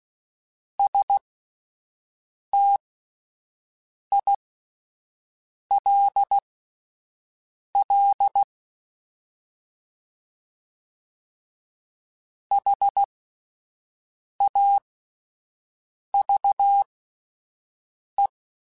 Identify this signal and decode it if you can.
Morse code, heard as a keyed tone, about 16 wpm, spaced as for 5 wpm, 800 Hz. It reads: STILL HAVE
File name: (spaced) / QSOs Spaced-mp3s / WA7QOX 1 @10 sp